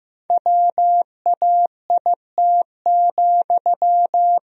Morse code, heard as a keyed tone, about 15 wpm, 700 Hz. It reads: WAIT,
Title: Sentences from Top 1000 Words - Review Part 4 - 15wpm